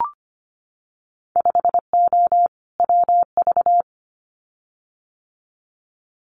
Morse code, heard as a keyed tone, about 25 wpm, 700 Hz. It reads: 5O WV
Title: Winter Field Day Exchanges Encore 25wpm